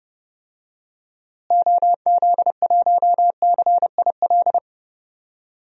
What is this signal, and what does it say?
Morse code, heard as a keyed tone, about 30 wpm, 700 Hz. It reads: OZ1CIL
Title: DXCC-100 Call Signs - 6 Characters - Repeated 2x - 30wpm